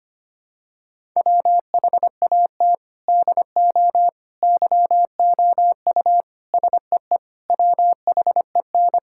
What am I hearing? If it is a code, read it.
Morse code, heard as a keyed tone, about 25 wpm, 700 Hz. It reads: WHAT DO YOU SEE WHEN
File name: practice / 25wpm / Sentences with 4-Word Phrases - 25wpm